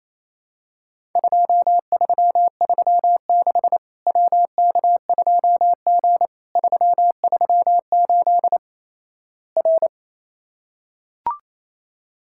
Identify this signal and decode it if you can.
Morse code, heard as a keyed tone, about 28 wpm, 700 Hz. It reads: 2336 WK2G 338 R
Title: Worked All Europe DX Contest - QTC Exchange - 28wpm